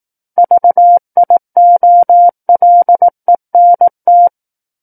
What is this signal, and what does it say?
Morse code, heard as a keyed tone, about 18 wpm, 700 Hz. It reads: VIOLENT
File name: 18 WPM